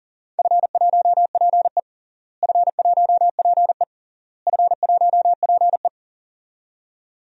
Morse code, heard as a keyed tone, about 40 wpm, 700 Hz. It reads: F1PE F1PE F1PE
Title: DXCC-100 Call Signs - 4 Characters - Repeated 3x - 40wpm